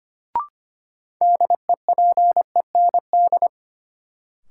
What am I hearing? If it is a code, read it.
Morse code, heard as a keyed tone, about 25 wpm, 700 Hz. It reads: DEPEND
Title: Sentences from Top 1300 Words - New Word Drill - 25wpm